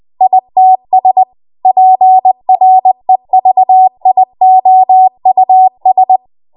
Morse code, heard as a keyed tone, about 20 wpm, 750 Hz. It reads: ITS PREVIOUS
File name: Morse / 20WPM